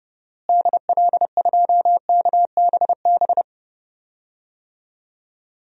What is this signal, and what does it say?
Morse code, heard as a keyed tone, about 30 wpm, 700 Hz. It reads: DL2KBB
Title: DXCC-100 Call Signs - 6 Characters - Repeated 3x - 30wpm